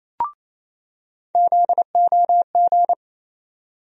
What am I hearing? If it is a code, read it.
Morse code, heard as a keyed tone, about 28 wpm, 700 Hz. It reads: ZOG
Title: USA Call Sign Postfixes - 3 Letters 28wpm